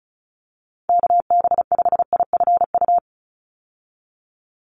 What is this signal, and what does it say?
Morse code, heard as a keyed tone, about 36 wpm, 700 Hz. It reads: KB5IFU
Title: USA Call Signs 35wpm